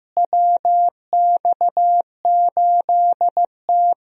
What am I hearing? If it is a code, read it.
Morse code, heard as a keyed tone, about 15 wpm, 700 Hz. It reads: WX8T
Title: USA Call Signs - 4 Characters - Repeated 3x - 15wpm